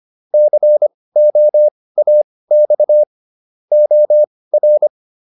Morse code, heard as a keyed tone, about 25 wpm, spaced as for 18 wpm, 600 Hz. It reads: COAX OR